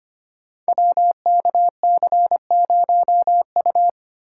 Morse code, heard as a keyed tone, about 25 wpm, 700 Hz. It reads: WKC0U
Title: Code Groups - 5 Letter:Numbers - 25wpm